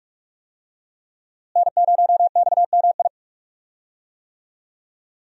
Morse code, heard as a keyed tone, about 47 wpm, 700 Hz. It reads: N0XMI
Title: USA Call Signs - 5 Characters - Repeated 2x - 45wpm